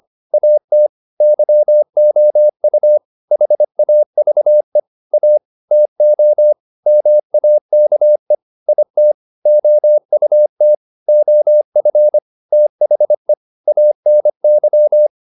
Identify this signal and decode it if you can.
Morse code, heard as a keyed tone, about 25 wpm, 600 Hz. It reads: AT YOU HAVE A TO MAKE IT OUT OF THE ANY